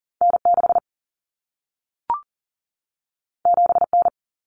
Morse code, heard as a keyed tone, about 40 wpm, 700 Hz. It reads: N6 7N
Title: Common Cut Numbers - 2-Digit 40wpm